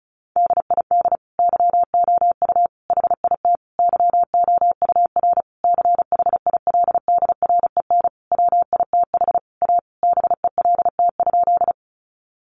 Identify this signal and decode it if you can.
Morse code, heard as a keyed tone, about 34 wpm, 700 Hz. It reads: DID YOU HIT YOUR CHILDREN WITH A BELT?
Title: Sentences from Top 1400 Words 35wpm